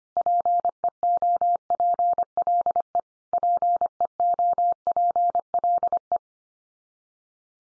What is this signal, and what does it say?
Morse code, heard as a keyed tone, about 25 wpm, 700 Hz. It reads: PEOPLE PEOPLE
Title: Top 100 Words 25wpm x3